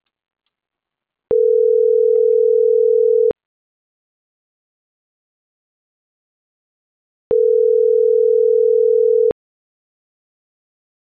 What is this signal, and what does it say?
Ringback tone (North America), 440 + 480 Hz.